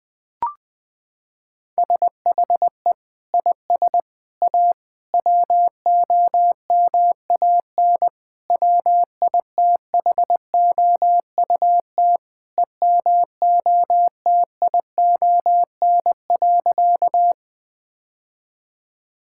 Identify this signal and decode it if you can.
Morse code, heard as a keyed tone, about 20 wpm, 700 Hz. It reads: SHE IS A WOMAN WITHOUT EMOTION.
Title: Sentences from Top 1600 Words 20wpm